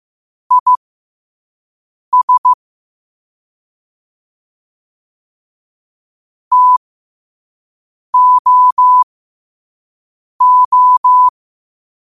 Morse code, heard as a keyed tone, about 15 wpm, spaced as for 5 wpm, 1000 Hz. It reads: IS TOO
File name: qotdfiles / 5wpm